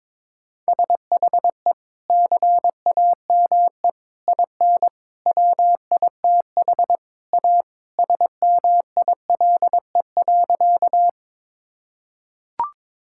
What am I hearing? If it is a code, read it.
Morse code, heard as a keyed tone, about 22 wpm, 700 Hz. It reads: SHE CAME IN WITH A SMILE.
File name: practice / 22wpm / Sentences with 2-Word Phrases - 22wpm